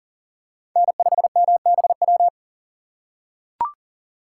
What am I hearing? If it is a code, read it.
Morse code, heard as a keyed tone, about 38 wpm, 700 Hz. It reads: N5MBW
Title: USA Call Signs - 5 Characters - 4x Character Spacing - 40wpm